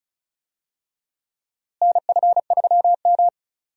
Morse code, heard as a keyed tone, about 36 wpm, 700 Hz. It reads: NF3M
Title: USA Call Signs - 4 Characters - Repeated 2x - 35wpm